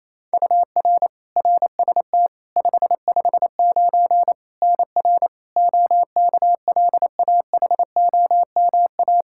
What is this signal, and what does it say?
Morse code, heard as a keyed tone, about 28 wpm, 700 Hz. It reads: UR RST 559 NR OKLAHOMA
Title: Practice QSO - Exchange 1 - Beginning - 28wpm